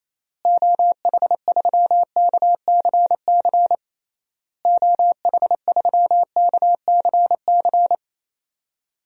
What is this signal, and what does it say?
Morse code, heard as a keyed tone, about 28 wpm, 700 Hz. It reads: OH3KCC OH3KCC
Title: DXCC-100 Call Signs - 6 Characters - Repeated 3x - 28wpm